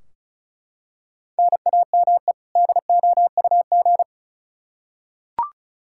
Morse code, heard as a keyed tone, about 36 wpm, 700 Hz. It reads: NAME DOUG
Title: Practice QSO - Exchange 1 - Beginning - 35wpm